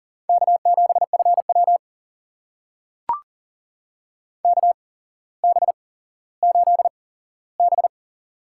Morse code, heard as a keyed tone, about 38 wpm, spaced as for 17 wpm, 700 Hz. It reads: K7FW KB8B
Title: USA Call Signs - 4 Characters - 8x Character Spacing - 40wpm